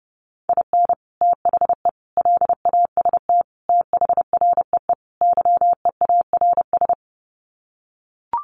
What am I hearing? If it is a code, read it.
Morse code, heard as a keyed tone, about 30 wpm, 700 Hz. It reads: IN THE LAST THREE YEARS
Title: Sets of 5 Words Encore2 30wpm